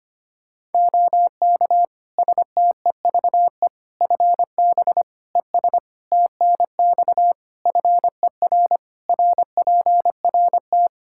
Morse code, heard as a keyed tone, about 25 wpm, 700 Hz. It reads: OK STEVE FB ES TNX FER RPRT
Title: Practice QSO - Exchange 2 - Info - 25wpm